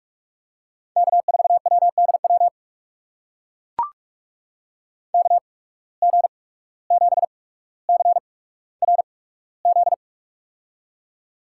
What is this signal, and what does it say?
Morse code, heard as a keyed tone, about 44 wpm, spaced as for 18 wpm, 700 Hz. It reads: K4WDW KG7CRZ
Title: USA Call Signs - 8x Character Spacing - 45wpm